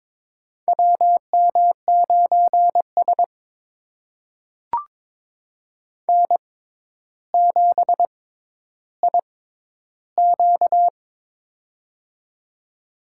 Morse code, heard as a keyed tone, about 22 wpm, spaced as for 11 wpm, 700 Hz. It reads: WM9S N7IQ
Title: USA Call Signs - 4 Characters - 6x Character Spacing - 22wpm